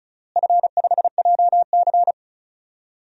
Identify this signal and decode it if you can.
Morse code, heard as a keyed tone, about 34 wpm, 700 Hz. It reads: F5JC
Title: DXCC-100 Call Signs - 4 Characters - Repeated 3x - 35wpm